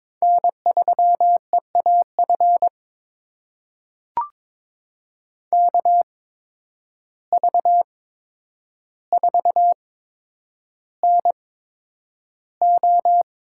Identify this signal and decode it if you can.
Morse code, heard as a keyed tone, about 22 wpm, spaced as for 9 wpm, 700 Hz. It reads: N3EAF KV4NO
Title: USA Call Signs - 5 Characters - 8x Character Spacing - 22wpm